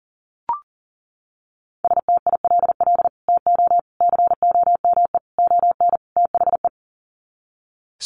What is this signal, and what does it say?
Morse code, heard as a keyed tone, about 38 wpm, 700 Hz. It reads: STILL TO COME ON THE
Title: Sets of 5 Words Encore2 40wpm